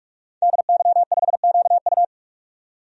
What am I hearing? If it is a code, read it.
Morse code, heard as a keyed tone, about 44 wpm, 700 Hz. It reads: DY5QU
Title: DXCC-100 Call Signs - 5 Characters - 45wpm